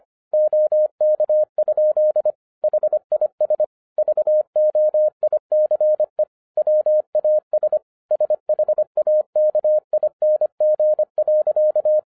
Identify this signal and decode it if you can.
Morse code, heard as a keyed tone, about 25 wpm, 610 Hz. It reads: OK? HIS VOICE WAS SHAKING.
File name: 2011-07-mp3 / n34 THE 3.11 2 25wpm